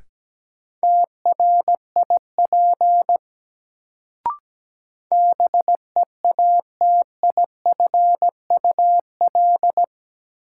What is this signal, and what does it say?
Morse code, heard as a keyed tone, about 17 wpm, 700 Hz. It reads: TRIP BEATIFUL
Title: Sentences from Top 1300 Words - New Word Drill Alt - 17wpm